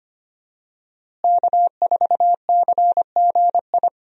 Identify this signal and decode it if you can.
Morse code, heard as a keyed tone, about 25 wpm, 700 Hz. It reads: K4CGI
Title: USA Call Signs - 5 Characters - Repeated 3x - 25wpm